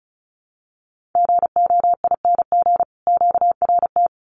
Morse code, heard as a keyed tone, about 34 wpm, 700 Hz. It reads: GOING QRT
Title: Common QSO Phrases 35wpm